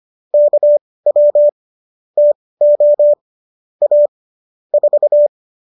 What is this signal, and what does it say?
Morse code, heard as a keyed tone, about 25 wpm, spaced as for 18 wpm, 590 Hz. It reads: KW TO A 4
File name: qso205 18